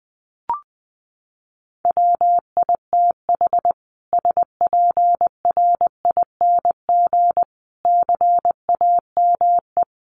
Morse code, heard as a keyed tone, about 19 wpm, 700 Hz. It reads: WITH SPRING CAME
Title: Sentences from Top 1200 Words 20wpm